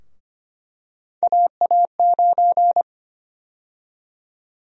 Morse code, heard as a keyed tone, about 25 wpm, 700 Hz. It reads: AA9
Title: USA Call Sign Prefixes - 25wpm